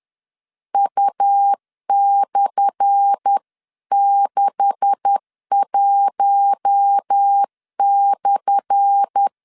Morse code, heard as a keyed tone, about 10 wpm, 800 Hz. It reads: U/61/
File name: LEZIONI / 10 SIGNS / 10A SLASH 10wpm 2808